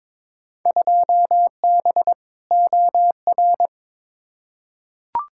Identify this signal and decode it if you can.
Morse code, heard as a keyed tone, about 21 wpm, 700 Hz. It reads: 2B OR
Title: Field Day Exchanges Encore - 22wpm